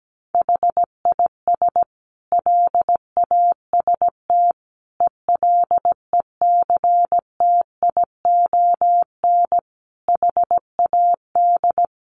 Morse code, heard as a keyed tone, about 17 wpm, 700 Hz. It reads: HIS LAST ELECTION HAD
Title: Sentences from Top 1500 Words 17wpm